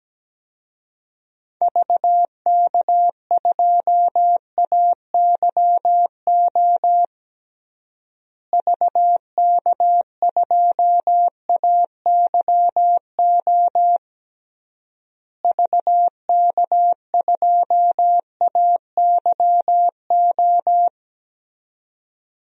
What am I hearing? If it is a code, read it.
Morse code, heard as a keyed tone, about 17 wpm, 700 Hz. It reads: VK2AYO VK2AYO VK2AYO
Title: DXCC-100 Call Signs - 6 Characters - Repeated 3x - 17wpm